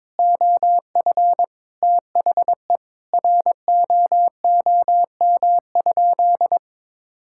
Morse code, heard as a keyed tone, about 22 wpm, 700 Hz. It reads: OF THE ROOM?
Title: Sentences from Top 1000 Words - Review Part 6 - 22wpm